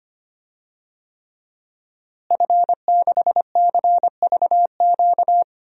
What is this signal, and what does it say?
Morse code, heard as a keyed tone, about 25 wpm, 700 Hz. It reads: F6CVQ
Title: DXCC-100 Call Signs - 5 Characters - Repeated 3x - 25wpm